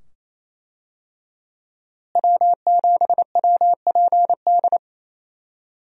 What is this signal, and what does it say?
Morse code, heard as a keyed tone, about 27 wpm, 700 Hz. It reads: W7WPD
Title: USA Call Signs - 5 Characters - Repeated 2x - 28wpm